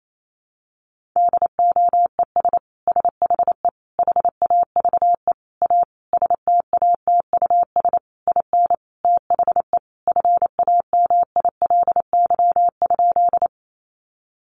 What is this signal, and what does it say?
Morse code, heard as a keyed tone, about 28 wpm, 700 Hz. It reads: DOES SHE HAVE A STATUS IN THE FAMILY?
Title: Sentences from Top 1400 Words 28wpm